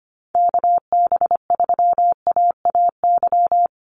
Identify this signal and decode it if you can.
Morse code, heard as a keyed tone, about 25 wpm, 700 Hz. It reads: KB3AAY
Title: Call Signs - 25wpm